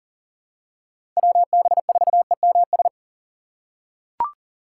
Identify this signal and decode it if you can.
Morse code, heard as a keyed tone, about 37 wpm, 700 Hz. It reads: WB4EMS
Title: USA Call Signs - 2x Character Spacing - 40wpm